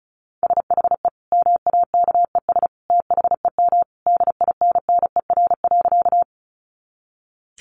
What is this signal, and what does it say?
Morse code, heard as a keyed tone, about 34 wpm, 700 Hz. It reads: SHE MAKES THEM DINNER.
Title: Easy Phrases 35wpm